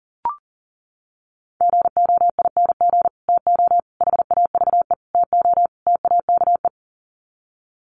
Morse code, heard as a keyed tone, about 40 wpm, 700 Hz. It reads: GOING TO HAVE TO TAKE
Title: Sets of 5 Words Encore 40wpm